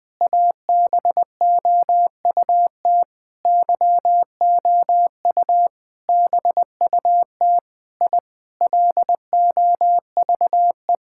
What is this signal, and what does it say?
Morse code, heard as a keyed tone, about 21 wpm, 700 Hz. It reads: ABOUT YOU BUT I LOVE
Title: Sentences with 7-Word Phrases - 20wpm